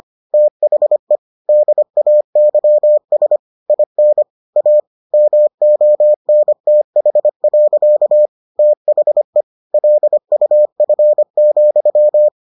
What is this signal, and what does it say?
Morse code, heard as a keyed tone, about 25 wpm, 600 Hz. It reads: THE DAYS IN A MONTH. THE LUF,